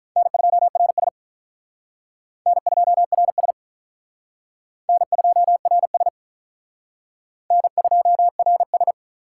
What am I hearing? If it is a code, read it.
Morse code, heard as a keyed tone, about 44 wpm, 700 Hz. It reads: N2RS N2RS N2RS N2RS
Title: Speed Racing - USA 1x2 Call Signs 35wpm